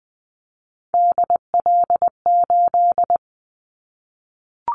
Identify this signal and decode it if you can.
Morse code, heard as a keyed tone, about 20 wpm, 700 Hz. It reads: DL8
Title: Callsign Prefixes for Top 50 DXCC - 20wpm